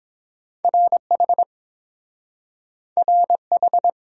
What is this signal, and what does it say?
Morse code, heard as a keyed tone, about 30 wpm, 700 Hz. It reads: RH RH